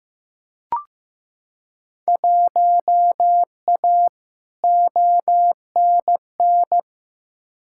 Morse code, heard as a keyed tone, about 15 wpm, 700 Hz. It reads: 1A ONN
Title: Field Day Exchanges Encore - 15wpm